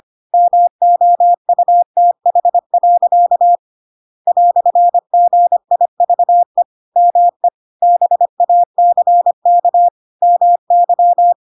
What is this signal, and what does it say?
Morse code, heard as a keyed tone, about 25 wpm, 700 Hz. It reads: MOUTH. "GIVE ME BACK MY
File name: Pinocchio / Pinocchio ch2 25 25 ik1hge